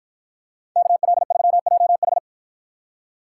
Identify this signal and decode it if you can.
Morse code, heard as a keyed tone, about 53 wpm, 700 Hz. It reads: KL3JH